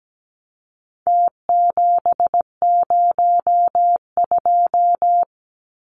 Morse code, heard as a keyed tone, about 17 wpm, 700 Hz. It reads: T702
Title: Numbers - Contest Style - 4-Digit 17wpm